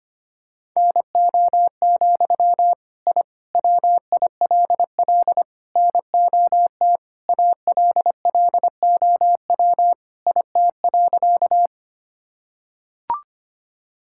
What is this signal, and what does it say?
Morse code, heard as a keyed tone, about 25 wpm, 700 Hz. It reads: NO, I WILL NOT ALLOW IT.
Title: Sentences from Top 1000 Words - Review Part 5 - 25wpm